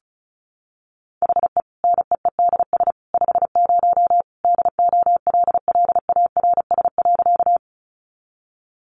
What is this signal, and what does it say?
Morse code, heard as a keyed tone, about 34 wpm, 700 Hz. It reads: HE NEEDS 50 DOLLARS.
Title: Additional Easy Phrases 35wpm